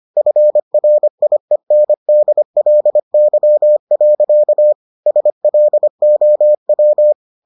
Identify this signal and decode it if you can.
Morse code, heard as a keyed tone, about 25 wpm, 600 Hz. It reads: FRIENDLY. SLOW